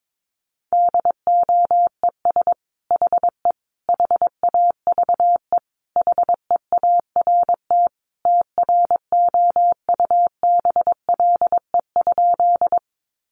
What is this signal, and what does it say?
Morse code, heard as a keyed tone, about 22 wpm, 700 Hz. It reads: DOES HE HAVE HEART TROUBLE?
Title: Sentences from Top 1200 Words 22wpm